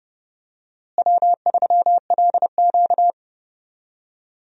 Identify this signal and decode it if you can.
Morse code, heard as a keyed tone, about 30 wpm, 700 Hz. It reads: W3LQ
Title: USA Call Signs - 4 Characters - Repeated 2x - 30wpm